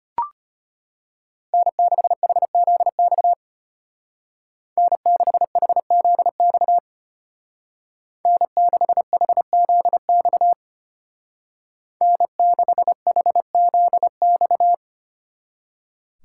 Morse code, heard as a keyed tone, about 30 wpm, 700 Hz. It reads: N6HZX N6HZX N6HZX N6HZX
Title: Speed Racing - USA 1x3 Call Signs 25wpm